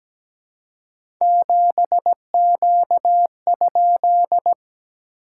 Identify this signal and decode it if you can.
Morse code, heard as a keyed tone, about 17 wpm, 700 Hz. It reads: 7Q?
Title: SOTA QSO Practice 17wpm